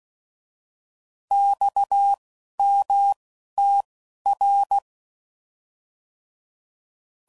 Morse code, heard as a keyed tone, about 16 wpm, spaced as for 12 wpm, 800 Hz. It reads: XMTR
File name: Mixed / Abreviations / Abreviations